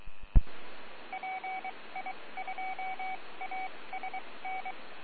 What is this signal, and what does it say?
Morse code, heard as a keyed tone, about 23 wpm, spaced as for 19 wpm, 760 Hz. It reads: PI2ASN